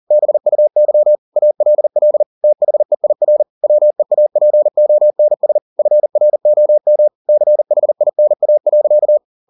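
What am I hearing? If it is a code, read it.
Morse code, heard as a keyed tone, about 40 wpm, 600 Hz. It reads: BUY ALL THEIR WEAPONS FROM CHINA.
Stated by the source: Morse code at 40 wpm, practice text, CW med Gnister.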